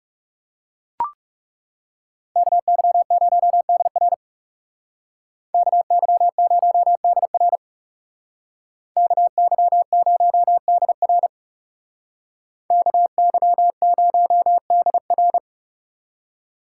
Morse code, heard as a keyed tone, about 39 wpm, 700 Hz. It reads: KY0DR KY0DR KY0DR KY0DR